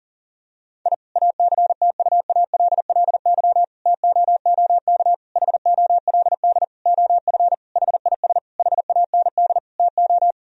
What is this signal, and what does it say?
Morse code, heard as a keyed tone, about 40 wpm, 700 Hz. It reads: I ACTUALLY TOOK HOLD OF HIS HAND TO
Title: Sentences from Top 1000 Words - Review Part 4 - 40wpm